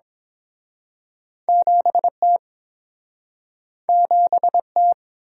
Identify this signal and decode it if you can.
Morse code, heard as a keyed tone, about 30 wpm, 700 Hz. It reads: 7 T 7 T